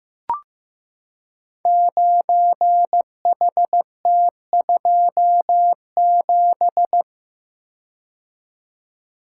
Morse code, heard as a keyed tone, about 15 wpm, 700 Hz. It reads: 9HT27
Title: Code Groups - 5 Letter:Numbers - 15wpm